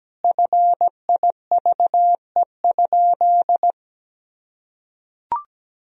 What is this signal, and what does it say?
Morse code, heard as a keyed tone, about 17 wpm, 700 Hz. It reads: FIVE?
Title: Sentences from Top 1000 Words - Review Part 2 - 17wpm